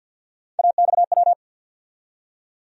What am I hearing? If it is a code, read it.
Morse code, heard as a keyed tone, about 53 wpm, 700 Hz. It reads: AXW
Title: USA Call Sign Postfixes - 3 Letters 50wpm